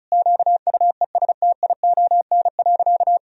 Morse code, heard as a keyed tone, about 34 wpm, 700 Hz. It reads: QUESTION.
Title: Sentences from Top 1000 Words - Review Part 2 - 35wpm